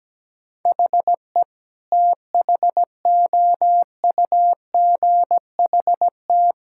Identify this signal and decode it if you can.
Morse code, heard as a keyed tone, about 17 wpm, 700 Hz. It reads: HE THOUGHT
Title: Sentences from Top 1600 Words 17wpm